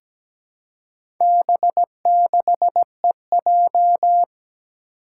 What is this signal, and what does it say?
Morse code, heard as a keyed tone, about 17 wpm, 700 Hz. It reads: B6EJ
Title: Four Challenging Letter-Number Combinations 17wpm